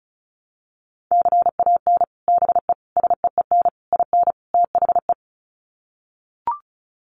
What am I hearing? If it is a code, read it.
Morse code, heard as a keyed tone, about 34 wpm, 700 Hz. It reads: CAN BE SEEN IN THE
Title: Sets of 5 Words Encore2 35wpm